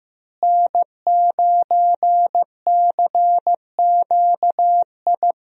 Morse code, heard as a keyed tone, about 15 wpm, 700 Hz. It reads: N9CQI
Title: USA Call Signs - 5 Characters - Repeated 2x - 15wpm